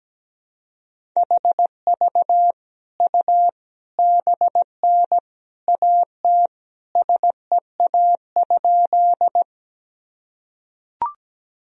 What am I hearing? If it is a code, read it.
Morse code, heard as a keyed tone, about 17 wpm, 700 Hz. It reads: HV U BN AT SEA?
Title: Sentences from Top 300 Words and Abbreviations 17wpm